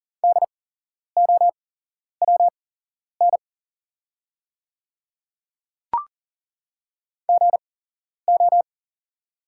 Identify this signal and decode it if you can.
Morse code, heard as a keyed tone, about 39 wpm, spaced as for 11 wpm, 690 Hz. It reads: DOWN GO
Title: 8x Character Spacing - Top 100 Words - 40wpm